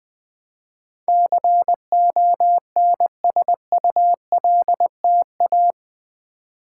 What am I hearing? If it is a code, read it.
Morse code, heard as a keyed tone, about 20 wpm, 700 Hz. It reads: CONSULTA
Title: Build A Word - 4 Letters At A Time - 12-Letter Words - 20wpm